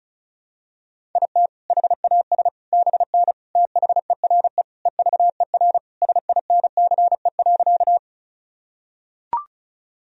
Morse code, heard as a keyed tone, about 36 wpm, 700 Hz. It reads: IT HAS BN THERE EVER SINCE.
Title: Sentences from Top 400 Words and Abbreviations 35wpm